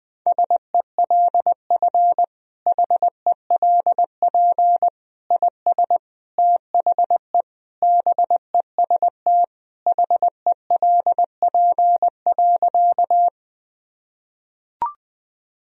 Morse code, heard as a keyed tone, about 21 wpm, 700 Hz. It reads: SELF HELP IS THE BEST HELP.
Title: Sentences from Top 1000 Words - Review Part 1 - 20wpm